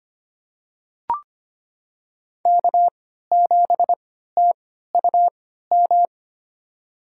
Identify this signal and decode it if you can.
Morse code, heard as a keyed tone, about 25 wpm, spaced as for 14 wpm, 700 Hz. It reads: K7TUM
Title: USA Call Signs - 3x Character Spacing - 25wpm